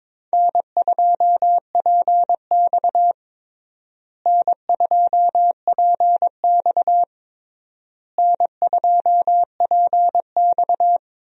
Morse code, heard as a keyed tone, about 22 wpm, 700 Hz. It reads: N2PX N2PX N2PX
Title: USA Call Signs - 4 Characters - Repeated 3x - 22wpm